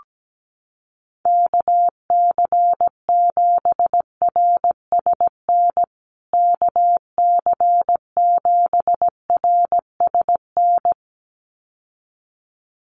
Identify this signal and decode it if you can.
Morse code, heard as a keyed tone, about 17 wpm, 700 Hz. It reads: KC7RSN KC7RSN
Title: USA Call Signs Repeated 2x 17wpm